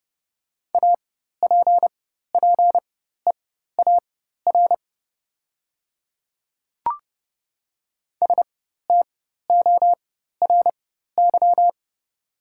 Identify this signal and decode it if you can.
Morse code, heard as a keyed tone, about 30 wpm, spaced as for 14 wpm, 700 Hz. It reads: APPEAR STORY
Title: 4x Character Spacing - Top 500 Words - 30wpm